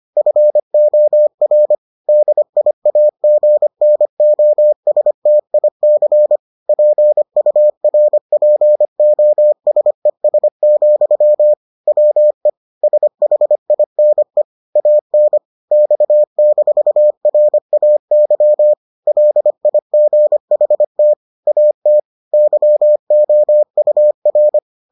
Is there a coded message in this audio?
Yes: Morse code from 0.2 to 24.6 s, heard as a keyed tone, about 25 wpm, 600 Hz. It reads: FOR DIAGNOSTIC PURPOSES, WE SHINE AN X-RAY LIGHT AT YOUR